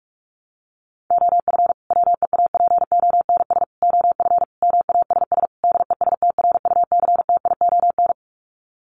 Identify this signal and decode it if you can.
Morse code, heard as a keyed tone, about 44 wpm, 700 Hz. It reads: OF WEAPONS OF MASS DESTRUCTION
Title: Sets of 5 Words Encore 45wpm